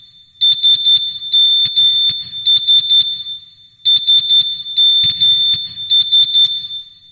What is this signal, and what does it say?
Morse code, heard as a keyed tone, about 11 wpm, 3550 Hz. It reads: SMS SMS